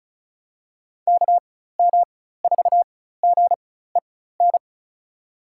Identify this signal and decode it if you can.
Morse code, heard as a keyed tone, about 36 wpm, spaced as for 16 wpm, 700 Hz. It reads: KM4GEN
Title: USA Call Signs - 6 Characters - 4x Character Spacing - 35wpm